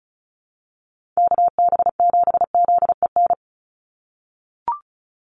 Morse code, heard as a keyed tone, about 34 wpm, 700 Hz. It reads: KB7ZEN
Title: USA Call Signs 35wpm